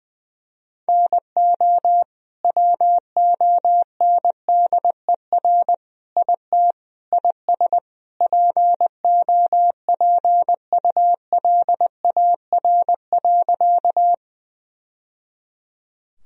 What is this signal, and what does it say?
Morse code, heard as a keyed tone, about 20 wpm, 700 Hz. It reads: NO WONDER IT IS POPULAR.
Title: Sentences from Top 1000 Words - Review Part 3 - 20wpm